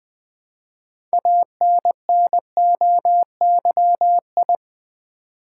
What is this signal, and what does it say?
Morse code, heard as a keyed tone, about 20 wpm, 700 Hz. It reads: ANNOYI
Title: Build A Word - 2 Letters At A Time - 8-Letter Words - 20wpm